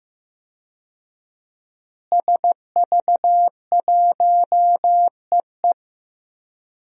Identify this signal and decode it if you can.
Morse code, heard as a keyed tone, about 15 wpm, 700 Hz. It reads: SV1EE
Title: DXCC-100 Call Signs - 5 Characters - Repeated 2x - 15wpm